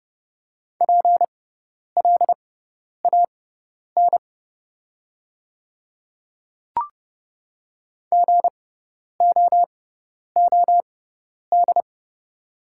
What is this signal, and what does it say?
Morse code, heard as a keyed tone, about 30 wpm, spaced as for 10 wpm, 700 Hz. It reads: PLAN GOOD
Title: 6x Character Spacing - Top 500 Words - 30wpm